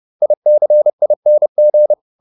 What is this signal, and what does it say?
Morse code, heard as a keyed tone, about 30 wpm, 600 Hz. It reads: ICING